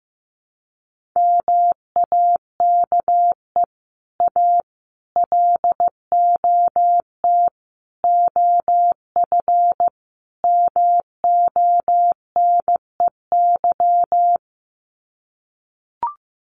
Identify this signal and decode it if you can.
Morse code, heard as a keyed tone, about 15 wpm, 700 Hz. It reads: MAKE A LOT OF MONEY
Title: Sets of 5 Words Encore2 15wpm